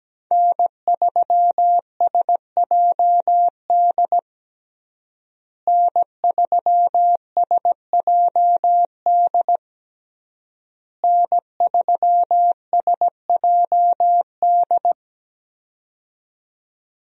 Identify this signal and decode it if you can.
Morse code, heard as a keyed tone, about 17 wpm, 700 Hz. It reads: N3SJD N3SJD N3SJD
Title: USA Call Signs - 5 Characters - Repeated 3x - 17wpm